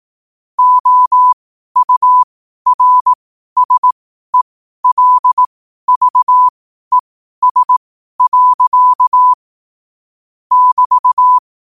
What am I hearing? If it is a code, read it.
Morse code, heard as a keyed tone, about 18 wpm, spaced as for 12 wpm, 1000 Hz. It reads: OURSELVES. <BT>
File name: qotdfiles / 12wpm